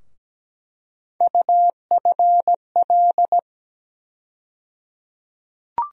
Morse code, heard as a keyed tone, about 17 wpm, 700 Hz. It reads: UFL
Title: USA Call Sign Postfixes - 3 Letters 17wpm